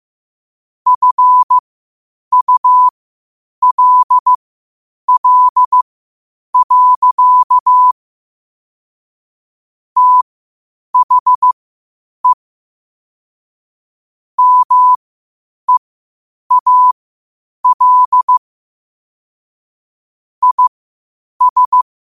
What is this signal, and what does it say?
Morse code, heard as a keyed tone, about 15 wpm, spaced as for 8 wpm, 1000 Hz. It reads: FULL. THE MEAL IS
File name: qotdfiles / 8wpm